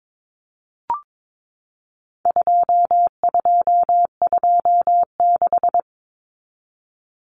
Morse code, heard as a keyed tone, about 22 wpm, 700 Hz. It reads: 2226
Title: Common Cut Numbers - 4-Digit 22wpm